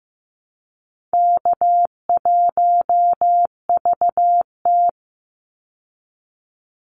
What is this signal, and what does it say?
Morse code, heard as a keyed tone, about 15 wpm, 700 Hz. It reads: K1VT
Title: Easy Call Signs Repeated 2x - 15wpm